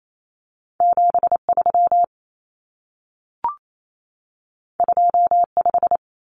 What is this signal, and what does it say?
Morse code, heard as a keyed tone, about 28 wpm, 700 Hz. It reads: 73 25
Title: Numbers - 2-Digit 28wpm